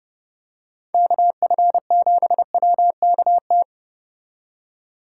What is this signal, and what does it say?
Morse code, heard as a keyed tone, about 30 wpm, 700 Hz. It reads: KF7WKT